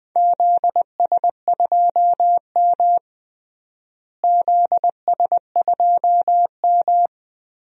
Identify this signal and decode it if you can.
Morse code, heard as a keyed tone, about 20 wpm, 700 Hz. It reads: ZS2M ZS2M
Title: DXCC-100 Call Signs - 4 Characters - Repeated 3x - 20wpm